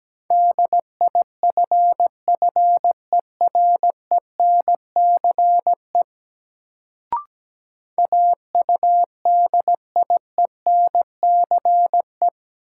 Morse code, heard as a keyed tone, about 17 wpm, 700 Hz. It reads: DIFFERENCE AUDIENCE
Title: Sentences from Top 800 Words - New Word Drill - 17wpm